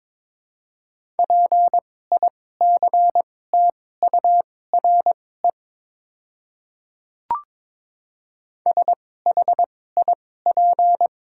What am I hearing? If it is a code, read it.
Morse code, heard as a keyed tone, about 23 wpm, spaced as for 16 wpm, 700 Hz. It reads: PICTURE SHIP
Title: 2x Character Spacing - Top 500 Words - 22wpm